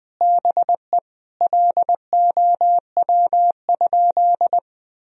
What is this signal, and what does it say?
Morse code, heard as a keyed tone, about 21 wpm, 700 Hz. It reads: BE LOW?
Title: Sentences from Top 1000 Words - Review Part 1 - 20wpm